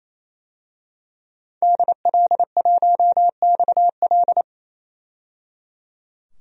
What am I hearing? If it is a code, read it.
Morse code, heard as a keyed tone, about 28 wpm, 700 Hz. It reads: DL1XL
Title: DXCC-100 Call Signs - 5 Characters - 28wpm